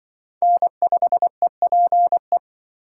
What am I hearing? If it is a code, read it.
Morse code, heard as a keyed tone, about 24 wpm, 700 Hz. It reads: N5EPE
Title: Speed Racing - USA 1x3 Call Signs 20wpm